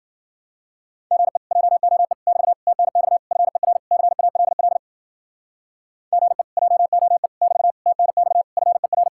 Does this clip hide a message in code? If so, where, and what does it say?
Yes: Morse code from 1.1 to 9.1 s, heard as a keyed tone, about 52 wpm, 700 Hz. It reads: GE JOE <BT> TNX FER CALL GE JOE <BT> TNX FER